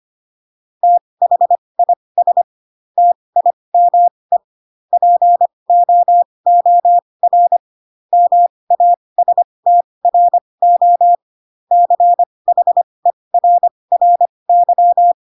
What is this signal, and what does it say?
Morse code, heard as a keyed tone, about 25 wpm, spaced as for 20 wpm, 700 Hz. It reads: THIS TIME POOR MASTRO CHERRY
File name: Pinocchio / Pinocchio ch1 15 25 ik1hge